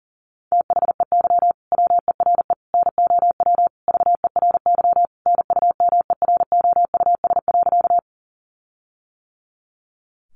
Morse code, heard as a keyed tone, about 40 wpm, 700 Hz. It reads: THEY WERE NOW VERY NUMEROUS.
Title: Sentences from Top 1400 Words 40wpm